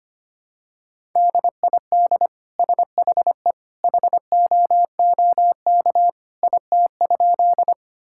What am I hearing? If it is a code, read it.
Morse code, heard as a keyed tone, about 25 wpm, 700 Hz. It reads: DID SHE HOOK IT?
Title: Sentences from Top 1600 Words 25wpm